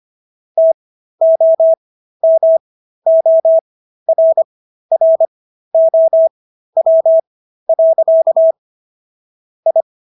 Morse code, heard as a keyed tone, about 25 wpm, spaced as for 13 wpm, 650 Hz. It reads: TOMORROW. I
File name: SS106 13